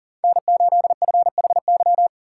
Morse code, heard as a keyed tone, about 40 wpm, 700 Hz. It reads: N8FHY
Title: USA Call Signs - 8x Character Spacing - 40wpm